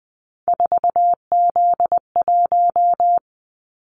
Morse code, heard as a keyed tone, about 20 wpm, 700 Hz. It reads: 4Z1
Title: Callsign Prefixes for Top 50 DXCC - 20wpm